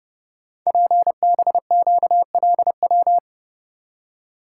Morse code, heard as a keyed tone, about 28 wpm, 700 Hz. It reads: PBQLW
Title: Code Groups - 5 Letters - 30wpm